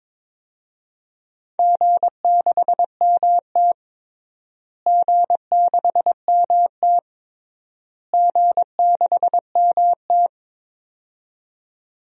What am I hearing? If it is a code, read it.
Morse code, heard as a keyed tone, about 22 wpm, 700 Hz. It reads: G6MT G6MT G6MT
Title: DXCC-100 Call Signs - 4 Characters - Repeated 3x - 22wpm